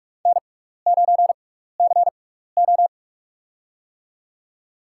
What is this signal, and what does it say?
Morse code, heard as a keyed tone, about 44 wpm, spaced as for 15 wpm, 700 Hz. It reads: N9CO
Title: USA Call Signs - 4 Characters - 6x Character Spacing - 45wpm